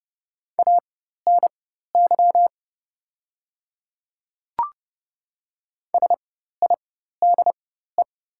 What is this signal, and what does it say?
Morse code, heard as a keyed tone, about 30 wpm, spaced as for 14 wpm, 700 Hz. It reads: ANY SIDE
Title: 4x Character Spacing - Top 200 Words - 30wpm